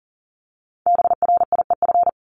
Morse code, heard as a keyed tone, about 42 wpm, 700 Hz. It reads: BRIEF
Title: Multi-letter Phonemes 40wpm